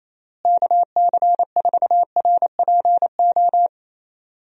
Morse code, heard as a keyed tone, about 28 wpm, 700 Hz. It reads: KC4RPO
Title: USA Call Signs - 6 Characters - Repeated 3x - 28wpm